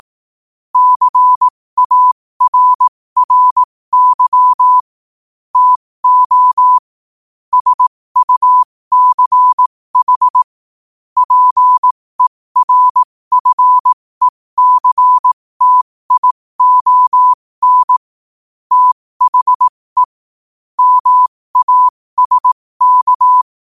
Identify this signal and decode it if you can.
Morse code, heard as a keyed tone, about 18 wpm, spaced as for 15 wpm, 1000 Hz. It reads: CARRY TO SUCH PERFECTION THE MASK